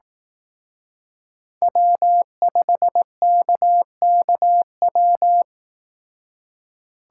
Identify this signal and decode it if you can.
Morse code, heard as a keyed tone, about 18 wpm, 700 Hz. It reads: W5KKW
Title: Speed Racing - USA 1x3 Call Signs 15wpm